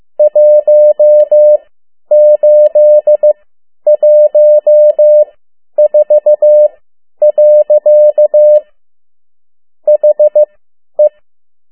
Morse code, heard as a keyed tone, about 15 wpm, spaced as for 10 wpm, 600 Hz. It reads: 1814. HE
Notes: Around the World in 80 Days paragraph one Morse code 10 WPM